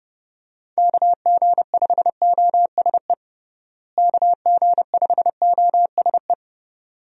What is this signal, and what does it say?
Morse code, heard as a keyed tone, about 30 wpm, 700 Hz. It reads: KG5OSE KG5OSE
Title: USA Call Signs - 6 Characters - Repeated 2x - 30wpm